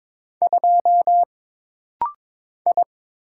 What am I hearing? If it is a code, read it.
Morse code, heard as a keyed tone, about 22 wpm, 700 Hz. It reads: 2 I
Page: This rapid-fire timing is geared to push your ICR (Instant Character Recognition) skills.